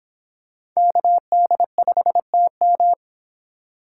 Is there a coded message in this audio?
Yes: Morse code from 0.8 to 2.9 s, heard as a keyed tone, about 25 wpm, 700 Hz. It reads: KD5TM